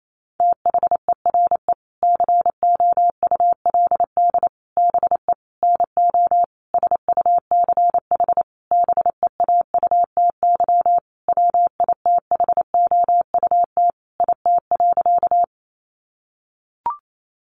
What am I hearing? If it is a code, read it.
Morse code, heard as a keyed tone, about 28 wpm, 700 Hz. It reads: THERE COULD BE NO SUCH BEAUTY WITHOUT IT.
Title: Sentences from Top 600 Words 28wpm